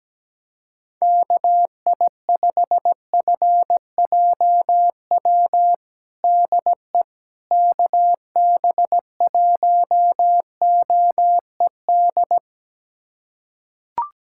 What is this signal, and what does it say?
Morse code, heard as a keyed tone, about 17 wpm, 700 Hz. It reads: KI5FJW DE KB1OED
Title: USA Callsign DE Callsign - 6 Characters - 17wpm